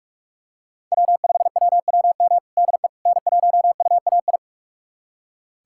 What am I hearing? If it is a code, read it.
Morse code, heard as a keyed tone, about 46 wpm, 700 Hz. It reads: W5WWM DE N1UAI
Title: USA Callsign DE Callsign - 5 Characters - 45wpm